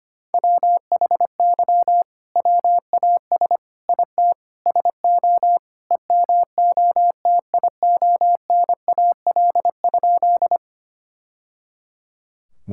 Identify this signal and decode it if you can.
Morse code, heard as a keyed tone, about 25 wpm, 700 Hz. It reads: WHY WAS IT SO EMOTIONAL?
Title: Sentences from Top 1000 Words - Review Part 5 - 25wpm